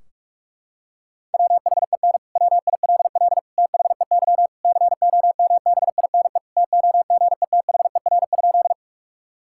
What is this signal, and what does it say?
Morse code, heard as a keyed tone, about 41 wpm, 700 Hz. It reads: WHEN WILL THEY COMBINE TOGETHER?
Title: Sentences from Top 1600 Words 45wpm